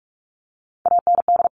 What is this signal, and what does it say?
Morse code, heard as a keyed tone, about 44 wpm, 700 Hz. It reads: AND
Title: Sentences from Top 200 Words 45wpm